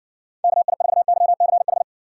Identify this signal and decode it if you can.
Morse code, heard as a keyed tone, about 60 wpm, 700 Hz. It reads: KI4YCH